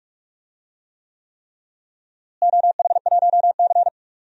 Morse code, heard as a keyed tone, about 46 wpm, 700 Hz. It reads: OH1C
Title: DXCC-100 Call Signs - 4 Characters - Repeated 3x - 45wpm